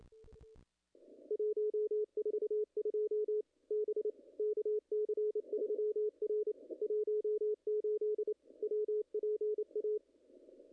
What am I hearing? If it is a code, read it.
Morse code, heard as a keyed tone, about 25 wpm, 410 Hz, with radio noise. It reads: 142 B KC3R 18 WPA
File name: contests / ss-cw-k5tr-op-n5rz -0100